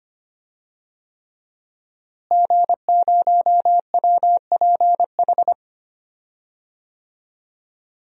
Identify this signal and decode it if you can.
Morse code, heard as a keyed tone, about 25 wpm, 700 Hz. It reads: G0WPH
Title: DXCC-100 Call Signs - 5 Characters - Repeated 3x - 25wpm